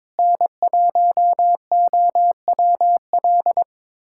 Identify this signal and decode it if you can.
Morse code, heard as a keyed tone, about 22 wpm, 700 Hz. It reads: N1OWL
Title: USA Callsign DE Callsign - 5 Characters - 22wpm